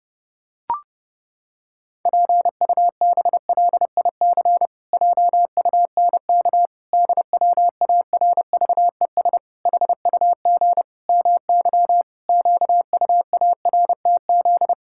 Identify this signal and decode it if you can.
Morse code, heard as a keyed tone, about 30 wpm, 700 Hz. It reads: PUBLIC JUNK DWARVES HUG MY QUARTZ
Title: Pangram Challenge v2 - 30wpm